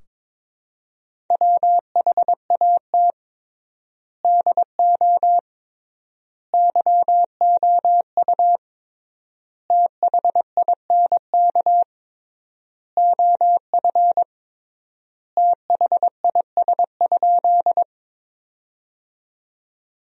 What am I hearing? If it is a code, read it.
Morse code, heard as a keyed tone, about 21 wpm, 700 Hz. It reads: WHAT DO YOU THINK OF THIS?
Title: 3x Word-Spacing - Sentences from Top 200 Words - 22wpm